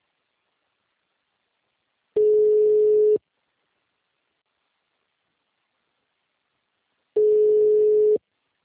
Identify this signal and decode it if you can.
Ringback tone (Europe, 425 Hz).